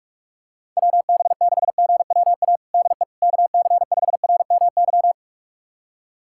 Morse code, heard as a keyed tone, about 46 wpm, 700 Hz. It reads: WB6GWA DE KC5RMY
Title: USA Callsign DE Callsign 45wpm